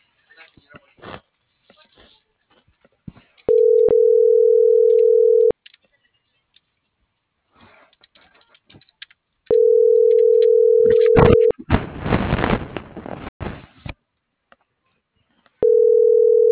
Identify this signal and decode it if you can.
Ringback tone (North America), 440 + 480 Hz.
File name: CENTRAL SAGOT / out diego.emir - .48427